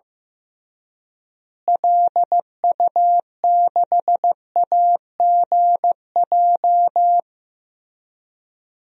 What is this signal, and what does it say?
Morse code, heard as a keyed tone, about 14 wpm, 700 Hz. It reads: LU6AGJ
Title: DXCC-100 Call Signs - 6 Characters - Repeated 3x - 15wpm